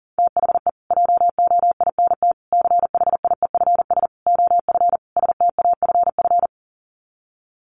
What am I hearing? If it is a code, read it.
Morse code, heard as a keyed tone, about 40 wpm, 700 Hz. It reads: THE JOINT CHIEFS OF STAFF
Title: Sets of 5 Words Encore2 40wpm